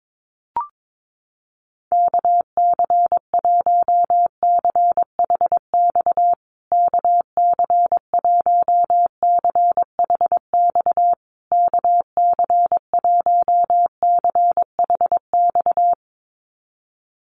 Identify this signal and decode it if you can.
Morse code, heard as a keyed tone, about 22 wpm, 700 Hz. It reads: KC1CHX KC1CHX KC1CHX
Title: USA Call Signs Repeated 3x 22wpm